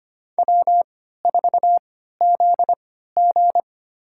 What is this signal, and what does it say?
Morse code, heard as a keyed tone, about 25 wpm, spaced as for 14 wpm, 700 Hz. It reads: W4ZG
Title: USA Call Signs - 3x Character Spacing - 25wpm